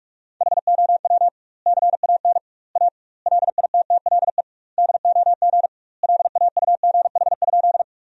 Morse code, heard as a keyed tone, about 46 wpm, 700 Hz. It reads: HOW CAN A LITTLE DOG LAUGH?
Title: 2x Word-Spacing - Sentences from Top 500 Words - 45wpm